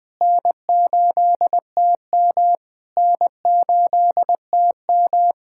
Morse code, heard as a keyed tone, about 21 wpm, 700 Hz. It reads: N8TM N8TM
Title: SOTA QSO Practice 20wpm